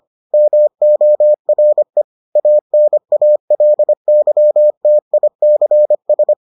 Morse code, heard as a keyed tone, about 25 wpm, 600 Hz. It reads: MORE ANALYTICS